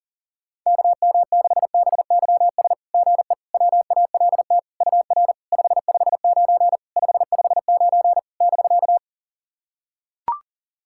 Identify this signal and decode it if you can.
Morse code, heard as a keyed tone, about 39 wpm, 690 Hz. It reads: KM6BYS GE WALT UR 559 559 <BK>
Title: SOTA QSO Practice 40wpm